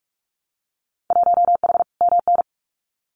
Morse code, heard as a keyed tone, about 46 wpm, 700 Hz. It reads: 1H MN
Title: Winter Field Day Exchanges Encore 45wpm